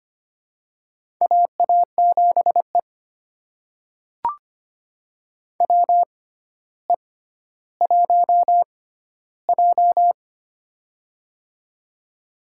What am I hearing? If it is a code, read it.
Morse code, heard as a keyed tone, about 25 wpm, spaced as for 13 wpm, 700 Hz. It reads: AA7E WE1J
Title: USA Call Signs - 4 Characters - 6x Character Spacing - 25wpm